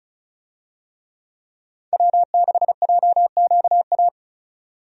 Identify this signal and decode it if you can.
Morse code, heard as a keyed tone, about 36 wpm, 700 Hz. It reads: W6JQA
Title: USA Call Signs - 5 Characters - 35wpm